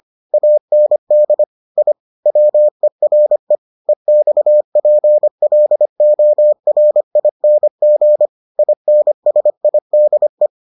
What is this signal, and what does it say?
Morse code, heard as a keyed tone, about 25 wpm, 600 Hz. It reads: AND I WERE EXPLORING INSIDE